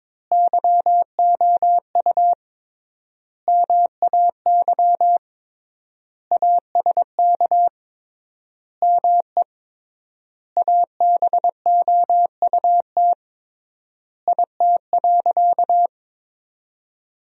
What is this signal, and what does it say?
Morse code, heard as a keyed tone, about 21 wpm, 700 Hz. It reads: YOU MAY ASK ME ABOUT IT.
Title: 3x Word-Spacing - Sentences from Top 200 Words - 22wpm